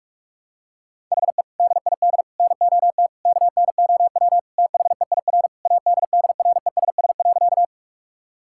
Morse code, heard as a keyed tone, about 41 wpm, 700 Hz. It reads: HE DID NOT KNOW THEIR ADDRESS.
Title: Sentences from Top 1000 Words - Review Part 6 - 45wpm